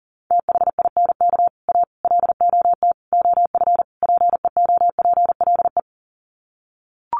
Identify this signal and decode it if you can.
Morse code, heard as a keyed tone, about 40 wpm, 700 Hz. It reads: THINK A LOT OF PEOPLE
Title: Sets of 5 Words 40wpm